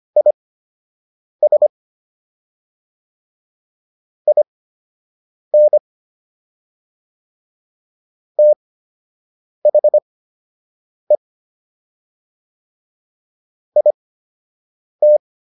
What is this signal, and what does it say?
Morse code, heard as a keyed tone, about 25 wpm, spaced as for 7 wpm, 600 Hz. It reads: IS IN THE IT